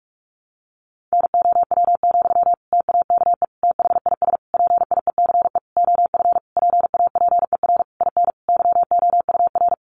Morse code, heard as a keyed tone, about 46 wpm, 700 Hz. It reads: NOW, TAKE THIS PIECE OF PAPER IN YOUR
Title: Sentences from Top 400 Words 45wpm